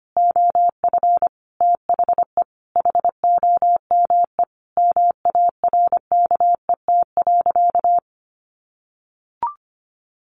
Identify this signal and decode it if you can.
Morse code, heard as a keyed tone, about 25 wpm, 700 Hz. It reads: OF THE HOME MARKET.
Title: Sentences from Top 1400 Words 25wpm